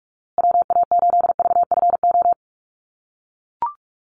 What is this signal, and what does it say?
Morse code, heard as a keyed tone, about 44 wpm, 700 Hz. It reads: WA8VFO
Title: USA Call Signs Repeated 3x 45wpm